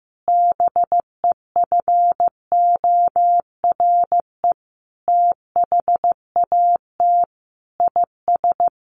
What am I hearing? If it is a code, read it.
Morse code, heard as a keyed tone, about 14 wpm, 700 Hz. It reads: BEFORE THAT IS
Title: Sentences from Top 1500 Words 15wpm